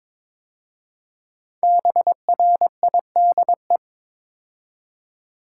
Morse code, heard as a keyed tone, about 22 wpm, 700 Hz. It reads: BRIDE
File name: 5 Letter Words - First 3 Letters Given - 22wpm